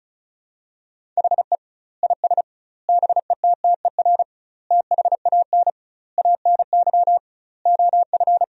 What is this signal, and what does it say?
Morse code, heard as a keyed tone, about 36 wpm, 700 Hz. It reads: HE IS BETTER THAN ANY OF
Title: 2x Word-Spacing - Sentences from Top 400 Words - 35wpm